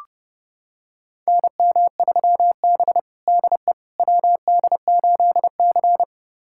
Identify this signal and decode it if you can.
Morse code, heard as a keyed tone, about 30 wpm, 700 Hz. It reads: NM3B DE WD8C
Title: USA Callsign DE Callsign - 4 Characters - 30wpm